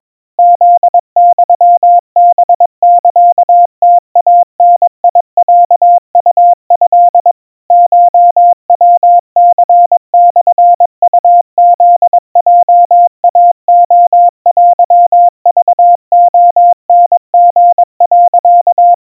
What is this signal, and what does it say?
Morse code (Japanese Wabun code), heard as a keyed tone, about 22 wpm, 700 Hz. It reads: フユハサムイダロウト コヤニモウフヲイレテクレタリ、
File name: 80 cpm 700 Hz Crispy